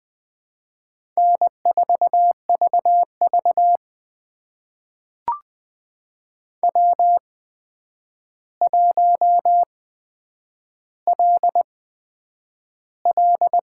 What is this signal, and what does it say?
Morse code, heard as a keyed tone, about 20 wpm, spaced as for 9 wpm, 700 Hz. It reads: N4VV W1LL
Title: USA Call Signs - 4 Characters - 8x Character Spacing - 20wpm